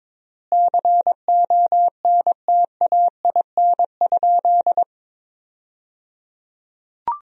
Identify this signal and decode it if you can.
Morse code, heard as a keyed tone, about 22 wpm, 700 Hz. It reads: CONTAIN?
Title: 3x Word-Spacing - Sentences from Top 500 Words - 22wpm